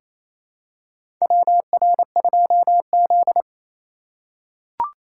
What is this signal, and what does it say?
Morse code, heard as a keyed tone, about 28 wpm, 700 Hz. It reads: WR2Z
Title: USA Call Signs - 2x Character Spacing - 28wpm